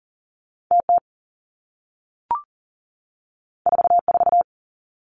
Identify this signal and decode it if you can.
Morse code, heard as a keyed tone, about 39 wpm, 700 Hz. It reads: TT 44
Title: Two Challenging Letter-Number Combinations 40wpm